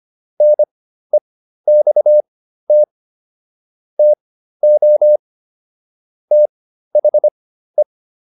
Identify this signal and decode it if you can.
Morse code, heard as a keyed tone, about 25 wpm, spaced as for 13 wpm, 600 Hz. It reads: NEXT TO THE